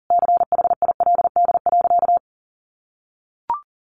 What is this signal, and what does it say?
Morse code, heard as a keyed tone, about 40 wpm, 700 Hz. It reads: CHILD.
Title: Sentences from Top 800 Words 40wpm